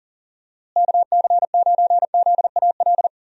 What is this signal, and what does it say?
Morse code, heard as a keyed tone, about 38 wpm, 700 Hz. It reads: KC9ZAL
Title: USA Call Signs - 6 Characters - Repeated 2x - 40wpm